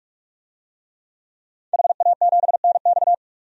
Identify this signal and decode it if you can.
Morse code, heard as a keyed tone, about 46 wpm, 700 Hz. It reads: HA7NX
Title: DXCC-100 Call Signs - 5 Characters - 45wpm